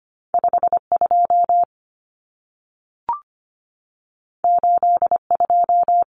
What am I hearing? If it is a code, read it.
Morse code, heard as a keyed tone, about 25 wpm, 700 Hz. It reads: 52 82
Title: Common Cut Numbers - 2-Digit 25wpm